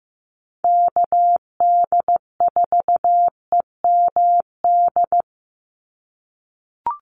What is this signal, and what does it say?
Morse code, heard as a keyed tone, about 15 wpm, 700 Hz. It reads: KD4EMD
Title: USA Call Signs 15wpm